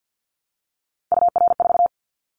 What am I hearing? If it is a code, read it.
Morse code, heard as a keyed tone, about 50 wpm, 700 Hz. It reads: UR4
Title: Callsign Prefixes for Top 25 DXCC - Repeated 3x - 50wpm